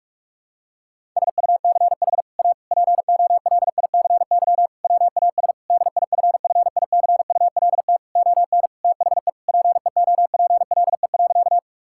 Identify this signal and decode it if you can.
Morse code, heard as a keyed tone, about 42 wpm, 700 Hz. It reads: SUCH A POLICY WAS DIFFICULT ON THE PEOPLE.
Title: Sentences from Top 1000 Words - Review Part 5 - 45wpm